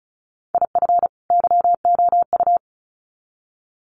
Morse code, heard as a keyed tone, about 34 wpm, 700 Hz. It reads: IF YOU
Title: Sets of 2 Words 35wpm